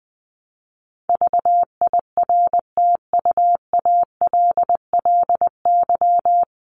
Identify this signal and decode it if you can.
Morse code, heard as a keyed tone, about 20 wpm, 700 Hz. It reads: VIRTUALLY
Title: Top 1501-1750 Words 20wpm